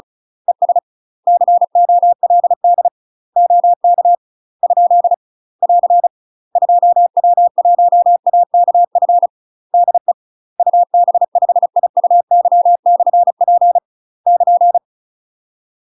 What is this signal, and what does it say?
Morse code, heard as a keyed tone, about 35 wpm, 700 Hz. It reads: ES COLD OK ? <AR> 2W1AKF DE UB5IUY/P <KN>
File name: UK MASTER QSOs TRAINING / 35WPM / UK MASTER QSOs 0028